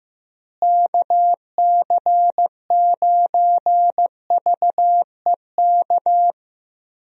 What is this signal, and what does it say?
Morse code, heard as a keyed tone, about 15 wpm, 700 Hz. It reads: KC9VEK
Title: USA Call Signs - 6 Characters - Repeated 3x - 15wpm